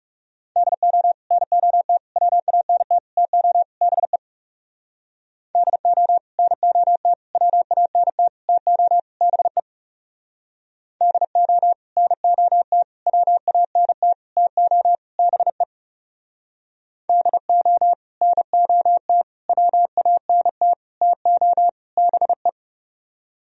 Morse code, heard as a keyed tone, about 39 wpm, 700 Hz. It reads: DO NOT WANT TO BE DO NOT WANT TO BE DO NOT WANT TO BE DO NOT WANT TO BE